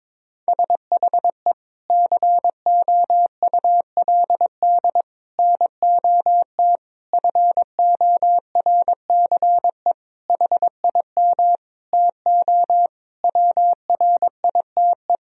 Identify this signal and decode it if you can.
Morse code, heard as a keyed tone, about 22 wpm, 700 Hz. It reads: SHE COULD NOT FORCE HIM TO WRITE
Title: Sentences from Top 1000 Words - Review Part 3 - 22wpm